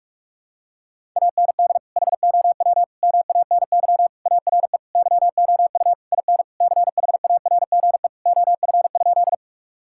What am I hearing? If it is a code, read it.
Morse code, heard as a keyed tone, about 41 wpm, 700 Hz. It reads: AND HOW MANY ARE YOU IN CHARGE OF?
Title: Sentences from Top 1000 Words - Review Part 6 - 45wpm